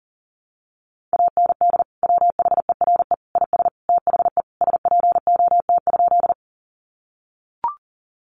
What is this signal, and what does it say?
Morse code, heard as a keyed tone, about 40 wpm, 700 Hz. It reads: AND WHERE IS THE SPOT?
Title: Sentences from Top 1100 Words 40wpm